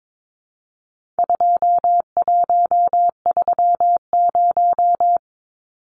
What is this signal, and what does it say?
Morse code, heard as a keyed tone, about 22 wpm, 700 Hz. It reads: 2130
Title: Numbers - 4-Digit 22wpm